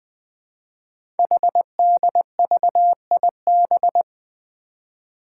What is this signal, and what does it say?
Morse code, heard as a keyed tone, about 20 wpm, 700 Hz. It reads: HDVIB
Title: Code Groups - 5 Letters - 20wpm